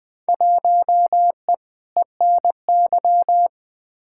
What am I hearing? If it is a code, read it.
Morse code, heard as a keyed tone, about 20 wpm, 700 Hz. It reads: 1E ENY
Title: Field Day Exchanges - 20wpm